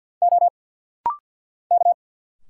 Morse code, heard as a keyed tone, about 50 wpm, 700 Hz. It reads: O K
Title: Single Letters - Rapid-Fire 50wpm